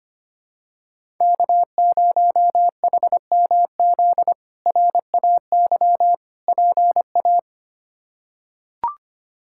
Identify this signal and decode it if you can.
Morse code, heard as a keyed tone, about 25 wpm, 700 Hz. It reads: K0HMZ RAY PA
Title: K1USN SST QSO Practice - 25wpm